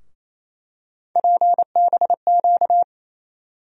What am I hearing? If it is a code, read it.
Morse code, heard as a keyed tone, about 28 wpm, 700 Hz. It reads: PBQ
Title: USA Call Sign Postfixes - 3 Letters 28wpm